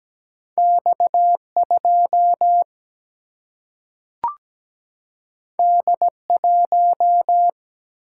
Morse code, heard as a keyed tone, about 17 wpm, 700 Hz. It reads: X2 D1
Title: Single-Letter Single-Number 17wpm